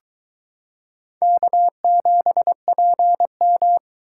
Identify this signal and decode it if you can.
Morse code, heard as a keyed tone, about 22 wpm, 700 Hz. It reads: K7PM